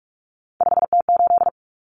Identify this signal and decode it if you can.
Morse code, heard as a keyed tone, about 44 wpm, 700 Hz. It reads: 5T8
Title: Common Cut Numbers - 3-Digit 45wpm